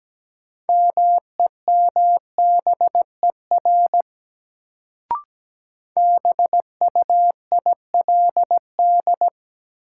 Morse code, heard as a keyed tone, about 17 wpm, 700 Hz. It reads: MEMBER BUILD
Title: Sentences from Top 1100 Words - New Word Drill Alt - 17wpm